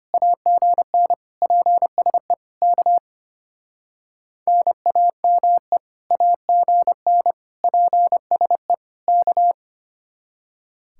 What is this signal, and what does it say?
Morse code, heard as a keyed tone, about 32 wpm, 700 Hz. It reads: AGN PSE K NAME AGN PSE K